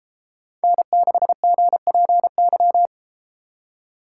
Morse code, heard as a keyed tone, about 32 wpm, 700 Hz. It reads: N6GPY